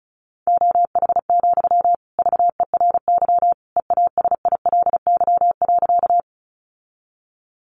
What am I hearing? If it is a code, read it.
Morse code, heard as a keyed tone, about 34 wpm, 700 Hz. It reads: OH, VERY EASILY.
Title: Sentences from Top 900 Words 35wpm